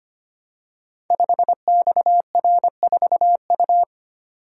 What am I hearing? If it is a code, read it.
Morse code, heard as a keyed tone, about 25 wpm, 700 Hz. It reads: 5XR4U
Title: Code Groups - 5 Letter:Numbers - 25wpm